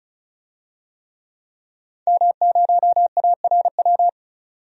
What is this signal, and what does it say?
Morse code, heard as a keyed tone, about 36 wpm, 700 Hz. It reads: M0ARW
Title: DXCC-100 Call Signs - 5 Characters - 35wpm